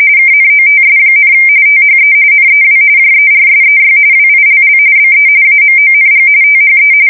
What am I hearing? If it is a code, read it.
RTTY, about 45.45 baud. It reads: KNOTS, THE NEW JERSEY WILL BE AMONG THE
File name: fsk170 / USS-New-Jersey